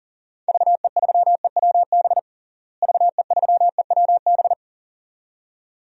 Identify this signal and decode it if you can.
Morse code, heard as a keyed tone, about 40 wpm, 700 Hz. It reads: VE3EWB VE3EWB
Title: DXCC-100 Call Signs - 6 Characters - Repeated 2x - 40wpm